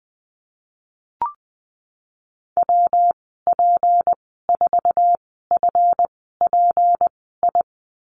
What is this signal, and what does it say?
Morse code, heard as a keyed tone, about 20 wpm, spaced as for 15 wpm, 700 Hz. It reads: WP4FPI
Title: 161 Lesson 26g - Letter g - Call Signs 20wpm